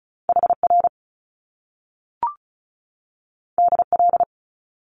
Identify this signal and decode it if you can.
Morse code, heard as a keyed tone, about 36 wpm, 700 Hz. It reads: HR DL
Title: Top Two Letter Combinations in Words 35wpm